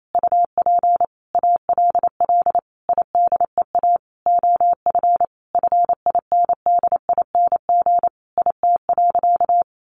Morse code, heard as a keyed tone, about 28 wpm, 700 Hz. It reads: UP ALL IDEA OF FINDING IT.
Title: Sentences from Top 1100 Words 28wpm